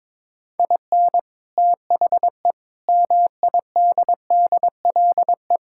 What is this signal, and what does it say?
Morse code, heard as a keyed tone, about 21 wpm, 700 Hz. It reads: IN THE MIDDLE
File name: practice / 22wpm / Sentences with 3-Word Phrases - 22wpm